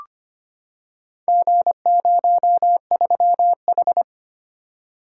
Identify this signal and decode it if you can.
Morse code, heard as a keyed tone, about 25 wpm, 700 Hz. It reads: G03H
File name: practice / 25wpm / Four Letter-Number Combinations - 25wpm